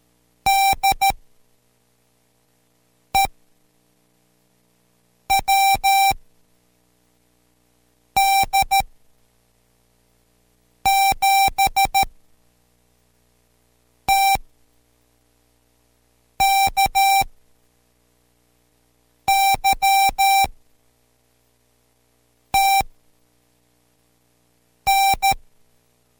Morse code, heard as a keyed tone, about 13 wpm, spaced as for 4 wpm, 780 Hz. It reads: DE WD7TKY TN